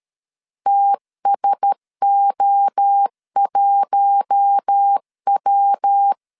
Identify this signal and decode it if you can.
Morse code, heard as a keyed tone, about 13 wpm, 800 Hz. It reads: TSO1W
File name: LEZIONI / 7 GF9Z / 7 G 12wpm 2055